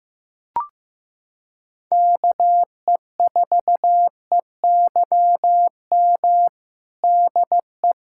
Morse code, heard as a keyed tone, about 16 wpm, 700 Hz. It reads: KE4EYM DE
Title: USA Callsign DE Callsign - 6 Characters - 15wpm